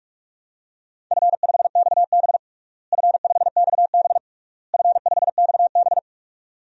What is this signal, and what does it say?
Morse code, heard as a keyed tone, about 46 wpm, 700 Hz. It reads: F5XB F5XB F5XB
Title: DXCC-100 Call Signs - 4 Characters - Repeated 3x - 45wpm